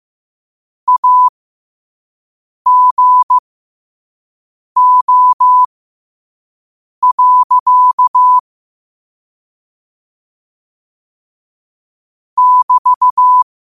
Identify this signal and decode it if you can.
Morse code, heard as a keyed tone, about 15 wpm, spaced as for 5 wpm, 1000 Hz. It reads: AGO. <BT>
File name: qotdfiles / 5wpm